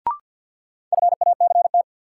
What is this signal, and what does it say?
Morse code, heard as a keyed tone, about 50 wpm, 700 Hz. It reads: FACT
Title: Sentences from Top 500 Words - New Word Drill - 50wpm